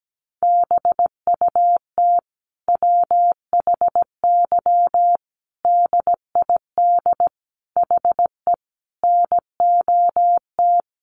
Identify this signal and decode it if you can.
Morse code, heard as a keyed tone, about 17 wpm, 700 Hz. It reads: BUT WHY DID HE NOT
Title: Sentences from Top 1300 Words 17wpm